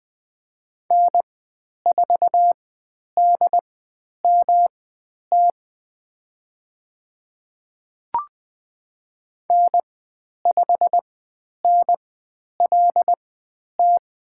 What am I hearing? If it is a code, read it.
Morse code, heard as a keyed tone, about 20 wpm, spaced as for 10 wpm, 700 Hz. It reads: N4DMT N5NLT
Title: 109 Lesson 18g - Letter m - Call Signs 20wpm